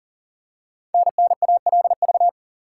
Morse code, heard as a keyed tone, about 40 wpm, 690 Hz. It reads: NNALV
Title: Code Groups - 5 Letter:Numbers - 40wpm